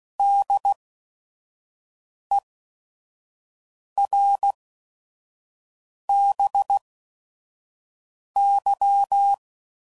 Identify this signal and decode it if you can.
Morse code, heard as a keyed tone, about 16 wpm, spaced as for 5 wpm, 800 Hz. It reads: DERBY
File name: CW Code Groups @5WPM / fromSM's / Cities